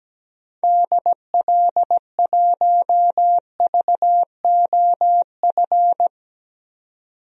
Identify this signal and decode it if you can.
Morse code, heard as a keyed tone, about 17 wpm, 700 Hz. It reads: DL1VOF
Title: DXCC-100 Call Signs - 6 Characters - 17wpm